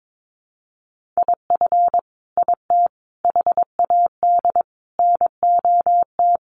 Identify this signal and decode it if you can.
Morse code, heard as a keyed tone, about 22 wpm, 700 Hz. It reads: IF IT HAD NOT
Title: Sets of 5 Words Encore 22wpm